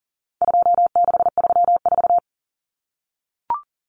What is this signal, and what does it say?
Morse code, heard as a keyed tone, about 40 wpm, 700 Hz. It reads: 2634
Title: Numbers - 4-Digit 40wpm